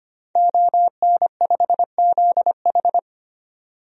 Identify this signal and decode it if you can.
Morse code, heard as a keyed tone, about 25 wpm, 700 Hz. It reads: ON5ZH
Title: DXCC-100 Call Signs - 5 Characters - 25wpm